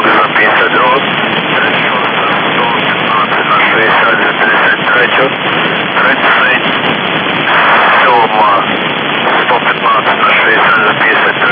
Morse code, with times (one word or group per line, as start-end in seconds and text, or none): none